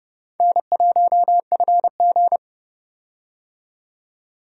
Morse code, heard as a keyed tone, about 31 wpm, 700 Hz. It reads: N1FG
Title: USA Call Signs - 4 Characters - 30wpm